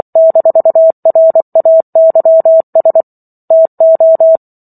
Morse code, heard as a keyed tone, about 24 wpm, 650 Hz. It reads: -RAYS TO
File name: Chapter 15 - 24 WPM